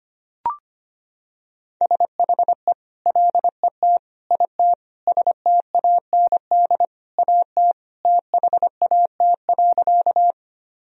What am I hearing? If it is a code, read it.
Morse code, heard as a keyed tone, about 25 wpm, 700 Hz. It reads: SHE LET IT STAND AT THAT.
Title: Sentences from Top 1000 Words - Review Part 1 - 25wpm